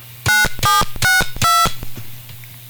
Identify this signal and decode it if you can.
DTMF, heard as telephone touch tones, tones of about 170 ms. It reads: 9762